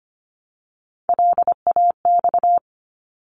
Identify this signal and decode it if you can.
Morse code, heard as a keyed tone, about 25 wpm, 700 Hz. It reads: LAX
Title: ARRL Sections 25wpm